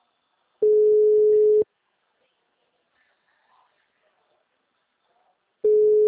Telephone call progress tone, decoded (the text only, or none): ringback tone